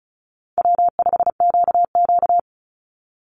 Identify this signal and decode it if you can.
Morse code, heard as a keyed tone, about 34 wpm, 700 Hz. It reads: W5QQ
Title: Call Signs Repeated 2x - 35wpm